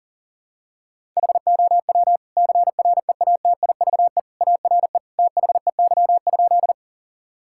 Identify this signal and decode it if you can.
Morse code, heard as a keyed tone, about 40 wpm, 700 Hz. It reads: HOW CREATIVE ARE THEY?
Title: Sentences from Top 1000 Words - Review Part 5 - 40wpm